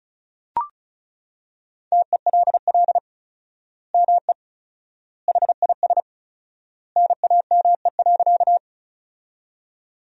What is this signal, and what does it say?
Morse code, heard as a keyed tone, about 36 wpm, 700 Hz. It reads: TELL ME HIS NAME.
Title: 4x Word-Spacing - Sentences from Top 200 Words - 35wpm